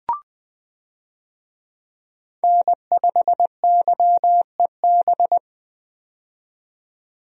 Morse code, heard as a keyed tone, about 21 wpm, 700 Hz. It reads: N5YEB
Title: USA Call Signs - 5 Characters - 20wpm